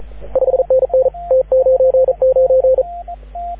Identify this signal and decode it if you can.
Morse code, heard as a keyed tone, about 35 wpm, 530 Hz, with radio noise. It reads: 5NN T99
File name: AUDIO2019 / HG6N CW SSB MIX